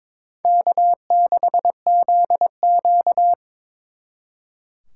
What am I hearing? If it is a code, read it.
Morse code, heard as a keyed tone, about 26 wpm, 700 Hz. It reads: K6ZQ
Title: Speed Racing - USA 1x2 Call Signs 22wpm